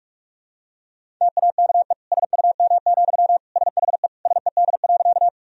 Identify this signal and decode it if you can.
Morse code, heard as a keyed tone, about 47 wpm, 700 Hz. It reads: TAKE SUM, SHE SED.
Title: Sentences from Top 200 Words and Abbreviations 45wpm